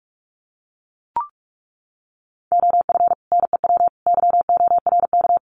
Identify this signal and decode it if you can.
Morse code, heard as a keyed tone, about 44 wpm, 700 Hz. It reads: OF NEW YORK
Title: Sets of 3 Words 45wpm